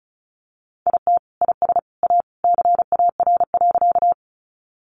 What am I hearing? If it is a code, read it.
Morse code, heard as a keyed tone, about 34 wpm, 700 Hz. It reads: IT IS A CAR.
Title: Easy Phrases 35wpm